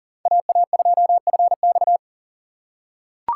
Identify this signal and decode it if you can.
Morse code, heard as a keyed tone, about 40 wpm, 700 Hz. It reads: AA2FX
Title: USA Call Signs - 5 Characters - 2x Character Spacing - 40wpm